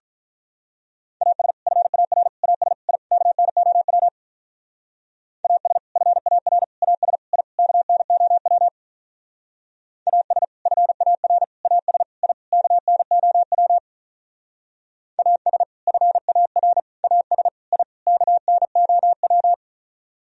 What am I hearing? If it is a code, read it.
Morse code, heard as a keyed tone, about 44 wpm, 700 Hz. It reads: AS FAR AS I KNOW AS FAR AS I KNOW AS FAR AS I KNOW AS FAR AS I KNOW
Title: Speed-Racing - Sets of 5 Words 35wpm